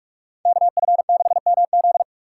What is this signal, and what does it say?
Morse code, heard as a keyed tone, about 44 wpm, 700 Hz. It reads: KF6MZ